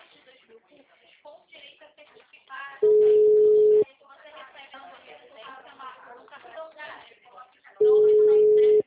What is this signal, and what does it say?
Ringback tone (Europe, 425 Hz), with speech.